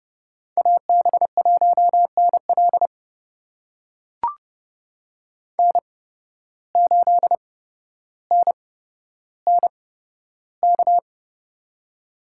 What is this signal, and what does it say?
Morse code, heard as a keyed tone, about 30 wpm, spaced as for 13 wpm, 700 Hz. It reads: AB1NL N8NNK
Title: USA Call Signs - 5 Characters - 8x Character Spacing - 30wpm